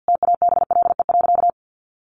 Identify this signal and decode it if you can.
Morse code, heard as a keyed tone, about 49 wpm, 700 Hz. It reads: TABLE.
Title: Sentences from Top 1400 Words 50wpm